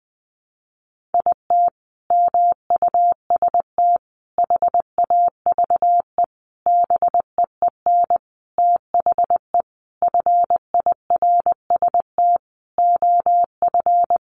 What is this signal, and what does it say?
Morse code, heard as a keyed tone, about 20 wpm, 700 Hz. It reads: IT MUST HAVE BEEN THE FIRST OF
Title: Sentences from Top 800 Words 20wpm